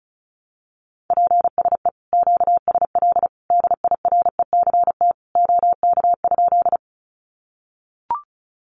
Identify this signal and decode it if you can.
Morse code, heard as a keyed tone, about 34 wpm, 700 Hz. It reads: PSE QSL DIRECT OK?
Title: Common QSO Phrases 35wpm